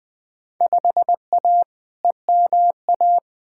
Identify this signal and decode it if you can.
Morse code, heard as a keyed tone, about 20 wpm, 700 Hz. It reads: 5A EMA
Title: Field Day Exchanges Encore - 20wpm